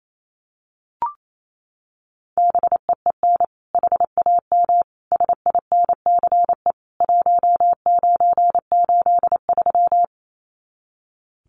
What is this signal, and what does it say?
Morse code, heard as a keyed tone, about 28 wpm, 700 Hz. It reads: BEEN HAM SINCE 1983
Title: Common QSO Phrases 28wpm